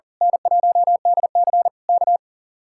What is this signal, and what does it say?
Morse code, heard as a keyed tone, about 39 wpm, 700 Hz. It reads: N1DC K
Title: Speed-Racing Common QSO Phrases - 40wpm